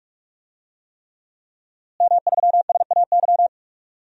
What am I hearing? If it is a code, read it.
Morse code, heard as a keyed tone, about 46 wpm, 700 Hz. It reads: M3SAY
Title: DXCC-100 Call Signs - 5 Characters - Repeated 3x - 45wpm